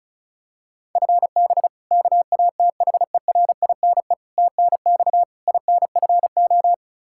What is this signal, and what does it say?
Morse code, heard as a keyed tone, about 36 wpm, 700 Hz. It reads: FB KATHERINE TNX INFO
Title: K1USN SST QSO Practice 35wpm